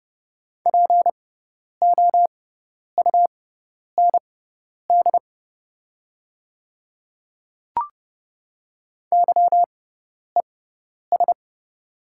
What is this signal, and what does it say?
Morse code, heard as a keyed tone, about 30 wpm, spaced as for 10 wpm, 700 Hz. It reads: POUND YES
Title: 6x Character Spacing - Top 500 Words - 30wpm